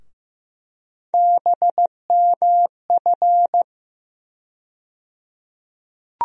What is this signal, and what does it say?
Morse code, heard as a keyed tone, about 15 wpm, 700 Hz. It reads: BMF
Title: USA Call Sign Postfixes - 3 Letters 15wpm